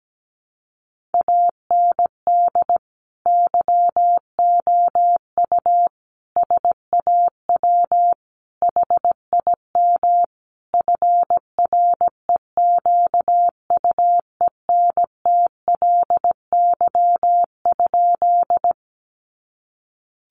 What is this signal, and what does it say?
Morse code, heard as a keyed tone, about 17 wpm, 700 Hz. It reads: AND YOU SAW HIM FREQUENTLY?
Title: Sentences from Top 1300 Words 17wpm